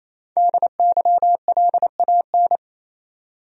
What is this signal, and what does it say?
Morse code, heard as a keyed tone, about 28 wpm, 700 Hz. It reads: DYLAN
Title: USA Common Names - Max 5 Letters - 28wpm